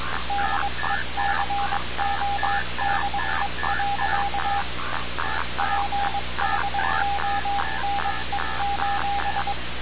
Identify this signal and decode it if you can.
Morse code, heard as a keyed tone, about 22 wpm, 800 Hz, with noise. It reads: DECODE. PHONE:
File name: radiocap / REC z